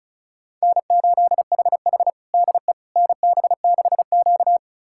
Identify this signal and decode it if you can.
Morse code, heard as a keyed tone, about 34 wpm, 700 Hz. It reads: N8HH DE NB6Q
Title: USA Callsign DE Callsign - 4 Characters - 35wpm